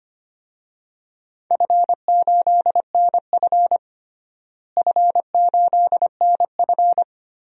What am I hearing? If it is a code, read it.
Morse code, heard as a keyed tone, about 25 wpm, 700 Hz. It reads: F8NF F8NF
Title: DXCC-100 Call Signs - 4 Characters - Repeated 3x - 25wpm